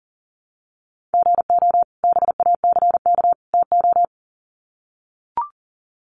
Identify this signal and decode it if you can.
Morse code, heard as a keyed tone, about 40 wpm, 700 Hz. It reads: GO BACK TO
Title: Sets of 3 Words 40wpm